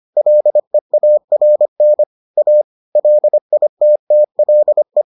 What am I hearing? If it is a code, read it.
Morse code, heard as a keyed tone, about 25 wpm, 600 Hz. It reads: LEARN A LITTLE